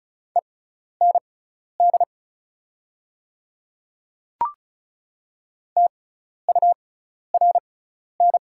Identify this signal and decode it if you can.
Morse code, heard as a keyed tone, about 36 wpm, spaced as for 12 wpm, 700 Hz. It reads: END TURN
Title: 6x Character Spacing - Top 300 Words - 35wpm